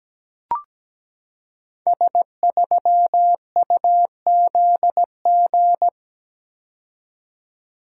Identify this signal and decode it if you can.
Morse code, heard as a keyed tone, about 17 wpm, 700 Hz. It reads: S3UZG
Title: Code Groups - 5 Letter:Numbers - 17wpm